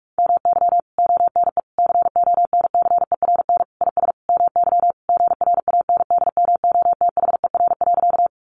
Morse code, heard as a keyed tone, about 44 wpm, 700 Hz. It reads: MY ONE CONCERN IS MY GRANDMOTHER.
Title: Sentences from Top 1200 Words 45wpm